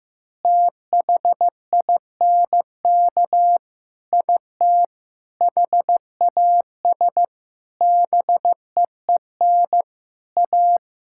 Morse code, heard as a keyed tone, about 15 wpm, 700 Hz. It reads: THINK IT HAS BEEN A
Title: Sentences from Top 1000 Words - Review Part 6 - 15wpm